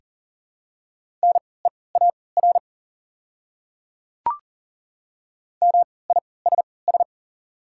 Morse code, heard as a keyed tone, about 40 wpm, spaced as for 23 wpm, 700 Hz. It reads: NEAR MISS
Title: 3x Character Spacing - Top 500 Words - 40wpm